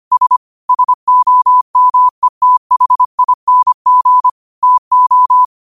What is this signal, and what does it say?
Morse code, heard as a keyed tone, about 25 wpm, 1000 Hz. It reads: S SOMETHING TO
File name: qotdfiles / 25wpm